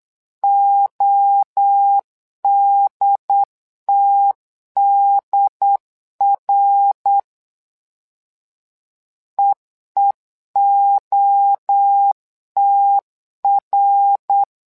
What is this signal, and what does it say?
Morse code, heard as a keyed tone, about 8 wpm, 800 Hz. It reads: ODTDR EEOTR
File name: LEZIONI / 3 MWRO / 3 MWRO 8wpm 2606